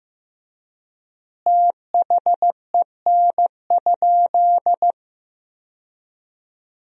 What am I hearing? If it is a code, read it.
Morse code, heard as a keyed tone, about 16 wpm, 700 Hz. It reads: THEN?
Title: 3x Word-Spacing - Sentences from Top 500 Words - 15wpm